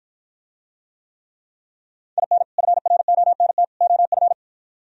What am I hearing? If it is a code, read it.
Morse code, heard as a keyed tone, about 52 wpm, 700 Hz. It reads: IN FRONT OF